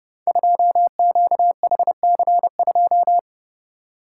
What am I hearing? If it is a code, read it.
Morse code, heard as a keyed tone, about 30 wpm, 700 Hz. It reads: 2QHC2
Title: Code Groups - 5 Letter:Numbers - 30wpm